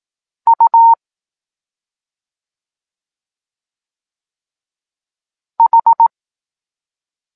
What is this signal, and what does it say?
Morse code, heard as a keyed tone, about 18 wpm, 920 Hz. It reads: U H
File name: Tekst-9-rownomierny